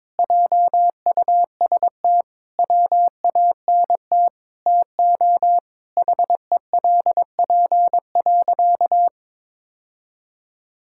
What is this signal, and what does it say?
Morse code, heard as a keyed tone, about 22 wpm, 700 Hz. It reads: JUST WANT TO HELP.
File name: practice / 22wpm / Sentences with 4-Word Phrases - 22wpm